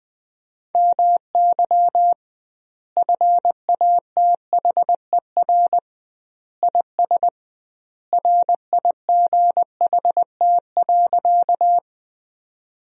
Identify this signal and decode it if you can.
Morse code, heard as a keyed tone, about 20 wpm, 700 Hz. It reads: MY FATHER IS RIGHT.
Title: 2x Word-Spacing - Sentences from Top 300 Words - 20wpm